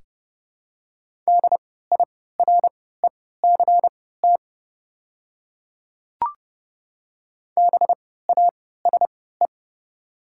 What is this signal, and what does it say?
Morse code, heard as a keyed tone, about 30 wpm, spaced as for 17 wpm, 690 Hz. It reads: DIRECT BASE
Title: 3x Character Spacing - Top 500 Words - 30wpm